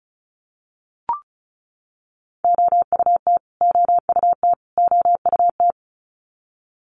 Morse code, heard as a keyed tone, about 34 wpm, 700 Hz. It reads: OUT OUT OUT
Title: Top 500 Words Repeated 35wpm